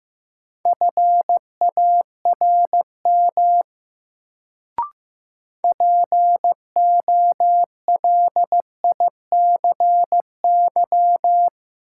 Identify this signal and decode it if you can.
Morse code, heard as a keyed tone, about 15 wpm, 700 Hz. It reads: FARM POLICY
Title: Sentences from Top 1200 Words - New Word Drill Alt - 15wpm